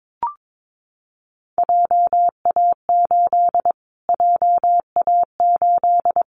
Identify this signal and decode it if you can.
Morse code, heard as a keyed tone, about 22 wpm, 700 Hz. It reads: JA8 JA8
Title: Callsign Prefixes for Top 25 DXCC - Repeated 3x - 22wpm